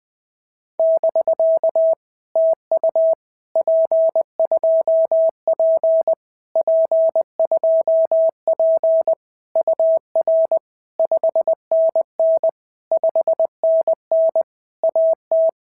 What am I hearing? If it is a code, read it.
Morse code, heard as a keyed tone, about 20 wpm, 650 Hz. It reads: <BK> TU P2P P2P UR 5NN 5NN AT
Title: POTA QSO Practice - 20wpm